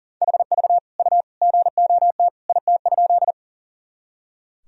Morse code, heard as a keyed tone, about 40 wpm, 700 Hz. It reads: HV U GOT IT?
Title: Sentences from Top 300 Words and Abbreviations 40wpm